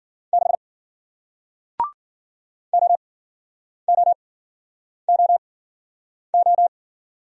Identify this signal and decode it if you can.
Morse code, heard as a keyed tone, about 54 wpm, spaced as for 9 wpm, 700 Hz. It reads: 6 OOOO